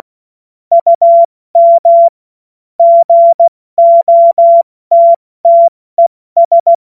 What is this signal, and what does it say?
Morse code, heard as a keyed tone, about 16 wpm, 680 Hz. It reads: UM GOTTES
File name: Morsekurs / Witze / Witz 11. Juli 2023 70 F80